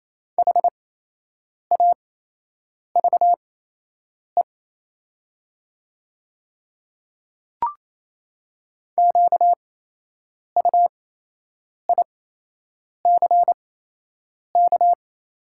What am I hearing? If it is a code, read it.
Morse code, heard as a keyed tone, about 28 wpm, spaced as for 8 wpm, 700 Hz. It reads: HAVE QUICK
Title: 8x Character Spacing - Top 500 Words - 28wpm